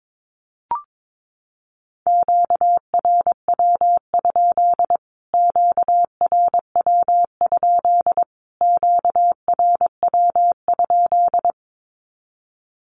Morse code, heard as a keyed tone, about 22 wpm, 700 Hz. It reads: QRW? QRW? QRW?
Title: Q-Codes Repeated 3x 22wpm